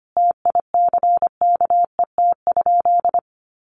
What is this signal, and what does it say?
Morse code, heard as a keyed tone, about 25 wpm, 700 Hz. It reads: TICKET?
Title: Sentences from Top 1400 Words 25wpm